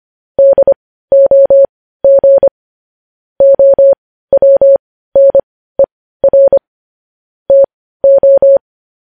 Morse code, heard as a keyed tone, about 25 wpm, spaced as for 15 wpm, 550 Hz. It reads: DOG OWNER TO
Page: news-headlines-in-morse-code-at-15-wpm